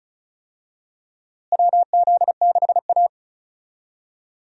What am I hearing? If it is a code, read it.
Morse code, heard as a keyed tone, about 36 wpm, 700 Hz. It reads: WZ6A
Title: USA Call Signs - 4 Characters - Repeated 2x - 35wpm